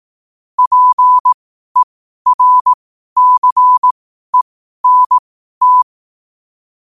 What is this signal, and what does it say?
Morse code, heard as a keyed tone, about 18 wpm, spaced as for 12 wpm, 1000 Hz. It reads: PERCENT
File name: qotdfiles / 12wpm